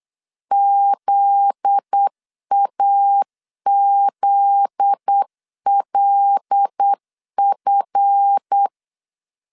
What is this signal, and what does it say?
Morse code, heard as a keyed tone, about 8 wpm, 800 Hz. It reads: ZAZLF
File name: LEZIONI / 7 GF9Z / 7 GF9Z 8wpm 3213